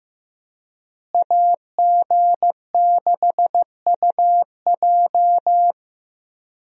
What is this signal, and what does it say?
Morse code, heard as a keyed tone, about 15 wpm, 700 Hz. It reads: AG6UJ
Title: USA Call Signs - 5 Characters - Repeated 2x - 15wpm